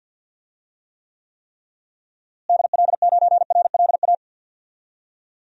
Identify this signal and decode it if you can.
Morse code, heard as a keyed tone, about 46 wpm, 700 Hz. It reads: DL9RLA
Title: DXCC-100 Call Signs - 6 Characters - Repeated 3x - 50wpm